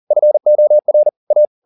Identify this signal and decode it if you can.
Morse code, heard as a keyed tone, about 40 wpm, 600 Hz. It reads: FOR A
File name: the old man and the sea 2 40wpm